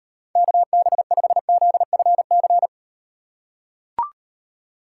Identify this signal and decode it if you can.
Morse code, heard as a keyed tone, about 38 wpm, 700 Hz. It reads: KB5ZFC